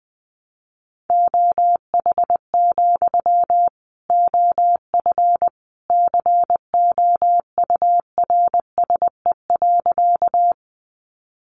Morse code, heard as a keyed tone, about 20 wpm, 700 Hz. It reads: OH, OF COURSE.
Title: Sentences from Top 500 Words 20wpm